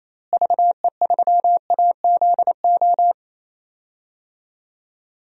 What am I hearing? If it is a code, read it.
Morse code, heard as a keyed tone, about 28 wpm, 700 Hz. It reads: VE3AZO
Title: DXCC-100 Call Signs - 6 Characters - Repeated 2x - 28wpm